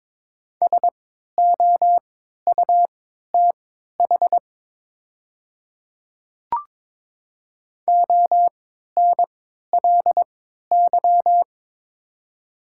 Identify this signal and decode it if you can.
Morse code, heard as a keyed tone, about 22 wpm, spaced as for 13 wpm, 700 Hz. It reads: SOUTH ONLY
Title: 3x Character Spacing - Top 500 Words - 22wpm